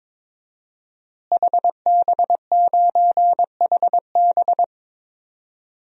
Morse code, heard as a keyed tone, about 22 wpm, 700 Hz. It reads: HB9HB
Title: DXCC-100 Call Signs - 5 Characters - Repeated 2x - 22wpm